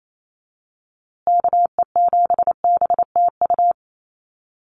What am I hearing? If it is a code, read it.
Morse code, heard as a keyed tone, about 28 wpm, 700 Hz. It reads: KE7BTU
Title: USA Call Signs Repeated 2x 28wpm